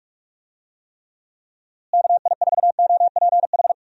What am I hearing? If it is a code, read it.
Morse code, heard as a keyed tone, about 42 wpm, 700 Hz. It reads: KI4OPH
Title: USA Call Signs - 6 Characters - Repeated 2x - 45wpm